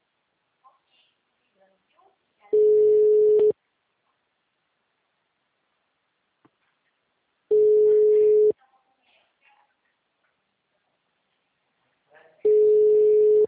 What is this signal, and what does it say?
Ringback tone (Europe, 425 Hz), with speech.